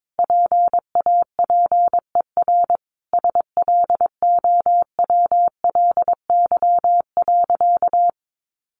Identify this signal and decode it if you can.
Morse code, heard as a keyed tone, about 22 wpm, 700 Hz. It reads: PAPER SLOWLY.
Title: Sentences from Top 1100 Words 22wpm